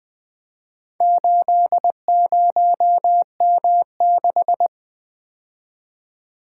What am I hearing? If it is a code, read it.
Morse code, heard as a keyed tone, about 20 wpm, 700 Hz. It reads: 80M6
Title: Four Letter-Number Combinations - 20wpm